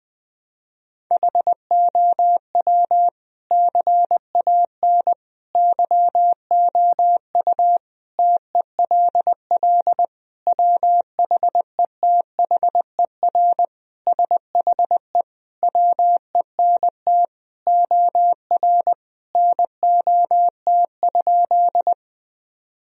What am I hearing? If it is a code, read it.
Morse code, heard as a keyed tone, about 20 wpm, 700 Hz. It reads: HOW CAN YOU TELL WHETHER SHE WENT OR NOT?
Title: Sentences from Top 1000 Words - Review Part 3 - 20wpm